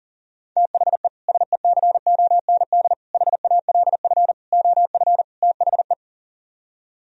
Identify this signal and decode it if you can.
Morse code, heard as a keyed tone, about 51 wpm, 700 Hz. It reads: THE SECOND HALF OF THE